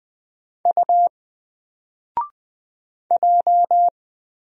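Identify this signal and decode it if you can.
Morse code, heard as a keyed tone, about 20 wpm, 700 Hz. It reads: U J
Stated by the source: Single letter and number Morse drill at 20 wpm, from N4LCD.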